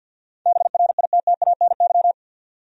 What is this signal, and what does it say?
Morse code, heard as a keyed tone, about 48 wpm, 700 Hz. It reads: BRITTANY
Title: USA Common Names - 50wpm